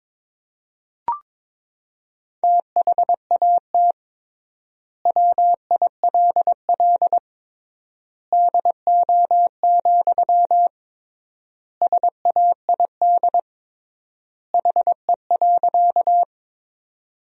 Morse code, heard as a keyed tone, about 22 wpm, 700 Hz. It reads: THAT WILL DO, SAID HE.
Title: 3x Word-Spacing - Sentences from Top 100 Words - 22wpm